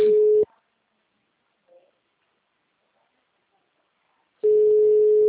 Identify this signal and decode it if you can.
Ringback tone (Europe, 425 Hz), with speech.